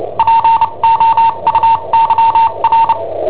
Morse code, heard as a keyed tone, about 29 wpm, spaced as for 25 wpm, 910 Hz. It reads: POUYR
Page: Cassetta esercizi preparata da I7OHP per IW5ELP